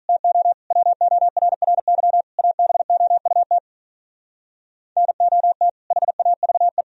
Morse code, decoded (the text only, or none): TO WORRY ABOUT NOT HAVE